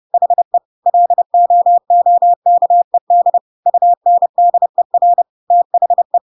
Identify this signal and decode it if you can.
Morse code, heard as a keyed tone, about 30 wpm, 700 Hz. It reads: HE LOOKED UNDER THE
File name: Pinocchio / Pinocchio ch1 30 30 ik1hge